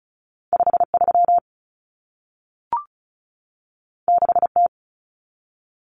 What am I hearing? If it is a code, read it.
Morse code, heard as a keyed tone, about 36 wpm, 700 Hz. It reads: 53 6T
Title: Common Cut Numbers - 2-Digit 35wpm